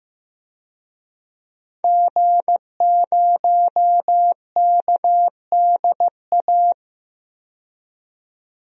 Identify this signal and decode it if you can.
Morse code, heard as a keyed tone, about 15 wpm, 700 Hz. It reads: G0KDA
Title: DXCC-100 Call Signs - 5 Characters - 15wpm